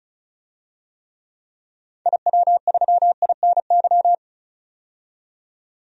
Morse code, heard as a keyed tone, about 34 wpm, 700 Hz. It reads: IW3INY
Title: DXCC-100 Call Signs - 6 Characters - 35wpm